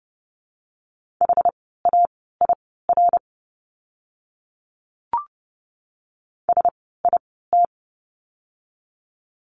Morse code, heard as a keyed tone, about 30 wpm, spaced as for 17 wpm, 700 Hz. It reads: HAIR SIT
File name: Lesson 17f - Letter u - Words 30wpm